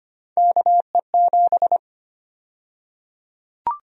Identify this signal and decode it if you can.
Morse code, heard as a keyed tone, about 25 wpm, 700 Hz. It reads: KE7
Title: USA Call Sign Prefixes - 25wpm